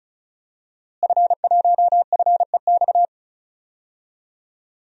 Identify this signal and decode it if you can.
Morse code, heard as a keyed tone, about 34 wpm, 700 Hz. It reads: F1FEX
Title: DXCC-100 Call Signs - 5 Characters - Repeated 2x - 35wpm